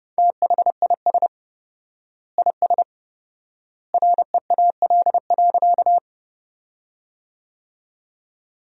Morse code, heard as a keyed tone, about 30 wpm, 700 Hz. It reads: THIS IS REAL.
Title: 4x Word-Spacing - Sentences from Top 300 Words - 30wpm